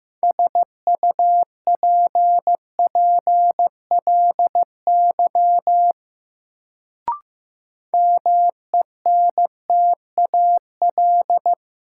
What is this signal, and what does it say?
Morse code, heard as a keyed tone, about 15 wpm, 700 Hz. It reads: SUPPLY MENTAL
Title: Sentences from Top 1000 Words - Review Part 4 - New Word Drill - 15wpm